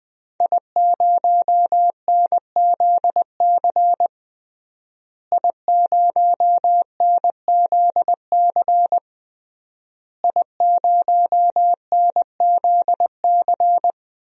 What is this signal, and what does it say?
Morse code, heard as a keyed tone, about 21 wpm, 700 Hz. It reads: I0NZC I0NZC I0NZC
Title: DXCC-100 Call Signs - 5 Characters - Repeated 3x - 20wpm